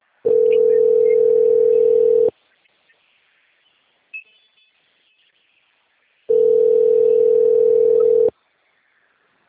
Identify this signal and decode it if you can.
Ringback tone (North America), 440 + 480 Hz.